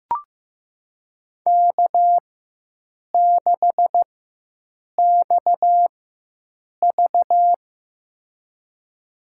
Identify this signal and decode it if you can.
Morse code, heard as a keyed tone, about 15 wpm, spaced as for 7 wpm, 700 Hz. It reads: K6XV
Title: USA Call Signs - 4 Characters - 4x Character Spacing - 15wpm